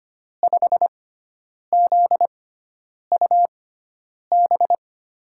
Morse code, heard as a keyed tone, about 25 wpm, spaced as for 9 wpm, 700 Hz. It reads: 5ZUB
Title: USA Call Signs - 6x Character Spacing - 25wpm